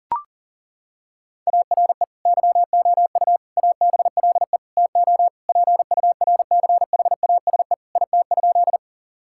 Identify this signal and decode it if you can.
Morse code, heard as a keyed tone, about 40 wpm, 700 Hz. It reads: ARE YOU ABLE TO PURCHASE IT?
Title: Sentences from Top 1000 Words - Review Part 5 - 40wpm